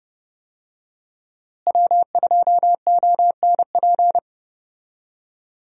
Morse code, heard as a keyed tone, about 30 wpm, 700 Hz. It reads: W2ONP
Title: USA Call Signs - 5 Characters - Repeated 3x - 30wpm